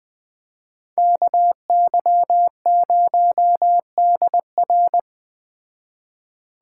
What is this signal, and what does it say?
Morse code, heard as a keyed tone, about 20 wpm, 700 Hz. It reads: KY0DR
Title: Speed Racing - USA Call Signs 17wpm